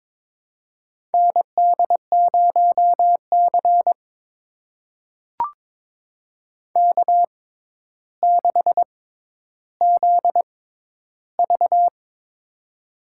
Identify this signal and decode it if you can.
Morse code, heard as a keyed tone, about 22 wpm, spaced as for 11 wpm, 700 Hz. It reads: ND0C K6ZV
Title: USA Call Signs - 4 Characters - 6x Character Spacing - 22wpm